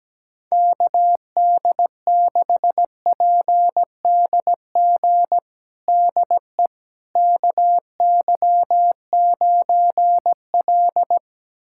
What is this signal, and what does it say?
Morse code, heard as a keyed tone, about 17 wpm, 700 Hz. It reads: KD6PDG DE KY9L
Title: USA Callsign DE Callsign 17wpm